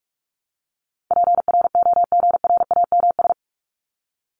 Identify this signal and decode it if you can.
Morse code, heard as a keyed tone, about 44 wpm, 700 Hz. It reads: PROGRAMS
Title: ES:S Suffix - Top 100 Words with Base Words 45wpm